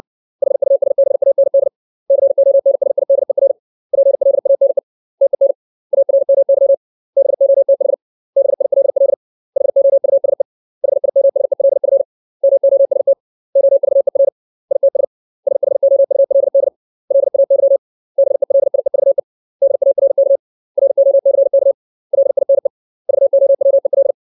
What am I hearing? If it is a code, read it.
Morse code, heard as a keyed tone, about 60 wpm, 550 Hz. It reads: HUSBAND CONSIDER PLANE TEN ARMY BOTH BILL HOUSE HIMSELF MOST OVER ITS SHOULD LAY BELIEVE DARK ROCK LINE VOWEL
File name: 632 words / 632 Words 60WPM